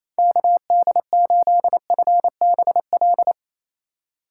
Morse code, heard as a keyed tone, about 27 wpm, 700 Hz. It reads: KD8FBL
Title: USA Call Signs - 6 Characters - 4x Character Spacing - 28wpm